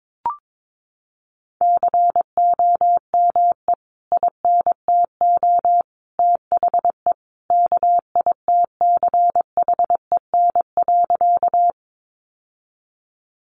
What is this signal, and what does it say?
Morse code, heard as a keyed tone, about 22 wpm, 700 Hz. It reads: COME INTO THE KITCHEN.
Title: Sentences from Top 1400 Words 22wpm